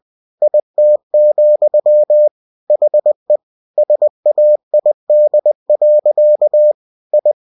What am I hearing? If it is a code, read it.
Morse code, heard as a keyed tone, about 20 wpm, 600 Hz. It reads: IT, HE SAID. I
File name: cw-books / winnie-the-pooh / winnie-20wpm-page-0087